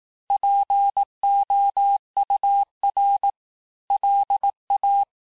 Morse code, heard as a keyed tone, about 18 wpm, 800 Hz. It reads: POUR LA
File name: cours-du-lundi files / cours-1852-texte-clair-vitesse-17MPM-puis-22MPM